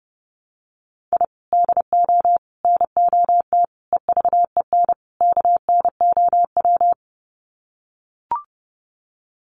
Morse code, heard as a keyed tone, about 30 wpm, 700 Hz. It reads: I DO NOT EVEN KNOW
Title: Sets of 5 Words 30wpm